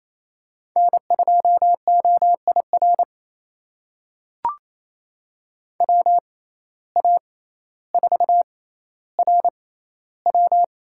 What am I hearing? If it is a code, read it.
Morse code, heard as a keyed tone, about 28 wpm, spaced as for 14 wpm, 700 Hz. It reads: N2OIR WA4RW
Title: USA Call Signs - 5 Characters - 6x Character Spacing - 28wpm